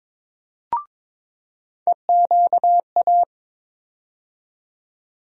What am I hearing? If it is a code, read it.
Morse code, heard as a keyed tone, about 22 wpm, 700 Hz. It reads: EQA
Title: USA Call Sign Postfixes - 3 Letters 22wpm